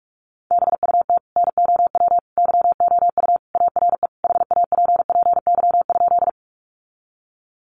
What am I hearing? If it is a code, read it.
Morse code, heard as a keyed tone, about 44 wpm, 700 Hz. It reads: BUT NOW YOU ARE HAPPY?
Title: Sentences from Top 1000 Words 45wpm